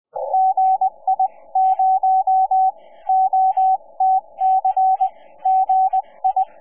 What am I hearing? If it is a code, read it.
Morse code, heard as a keyed tone, about 20 wpm, spaced as for 17 wpm, 760 Hz. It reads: GI0OTC GI
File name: 6M audio / k0kp otc